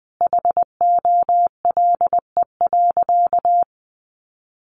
Morse code, heard as a keyed tone, about 20 wpm, 700 Hz. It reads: HOLE.
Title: Easy Phrases 20wpm